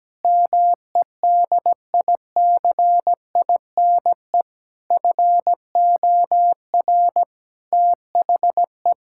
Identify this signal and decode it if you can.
Morse code, heard as a keyed tone, about 17 wpm, 700 Hz. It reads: MEDICINE FOR THE
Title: Sentences from Top 1000 Words - Review Part 6 - 17wpm